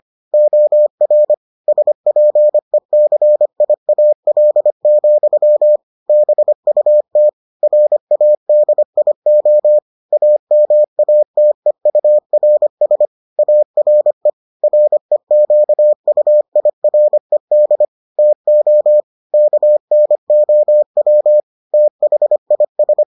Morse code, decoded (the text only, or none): OR SPECIAL, BUT RADIO AMATEURS ARE REQUIRED TO KNOW THIS